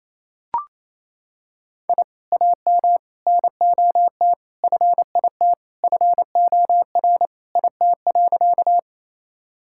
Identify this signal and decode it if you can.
Morse code, heard as a keyed tone, about 28 wpm, 700 Hz. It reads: I AM NOT FIT FOR IT.
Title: Sentences from Top 1000 Words - Review Part 5 - 28wpm